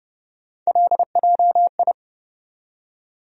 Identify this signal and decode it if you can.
Morse code, heard as a keyed tone, about 32 wpm, 700 Hz. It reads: LJI
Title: USA Call Sign Postfixes - 3 Letters 30wpm